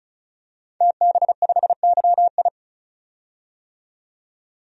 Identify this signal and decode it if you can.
Morse code, heard as a keyed tone, about 35 wpm, 700 Hz. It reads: TB5YI
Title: DXCC-100 Call Signs - 5 Characters - 35wpm